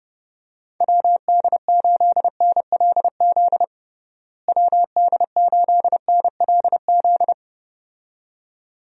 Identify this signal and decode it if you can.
Morse code, heard as a keyed tone, about 30 wpm, 700 Hz. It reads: WD8NLZ WD8NLZ
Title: USA Call Signs - 6 Characters - Repeated 2x - 30wpm